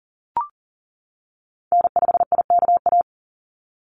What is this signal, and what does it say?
Morse code, heard as a keyed tone, about 40 wpm, 700 Hz. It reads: N5IKA
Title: USA Call Signs 40wpm